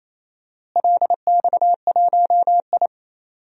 Morse code, heard as a keyed tone, about 28 wpm, 700 Hz. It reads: LX1I
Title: DXCC-100 Call Signs - 4 Characters - Repeated 2x - 28wpm